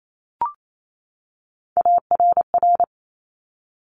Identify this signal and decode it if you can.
Morse code, heard as a keyed tone, about 28 wpm, 700 Hz. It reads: ARR
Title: Top Three Letter Combinations in Words 28wpm